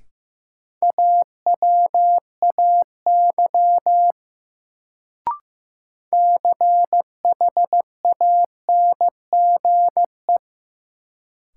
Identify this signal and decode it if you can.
Morse code, heard as a keyed tone, about 15 wpm, 700 Hz. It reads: AWAY CHANGE
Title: Sentences from Top 600 Words - New Word Drill - 15wpm